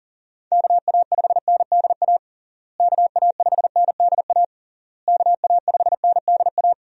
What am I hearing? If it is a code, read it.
Morse code, heard as a keyed tone, about 40 wpm, 700 Hz. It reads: KA5NDA KA5NDA KA5NDA
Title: USA Call Signs - 6 Characters - Repeated 3x - 40wpm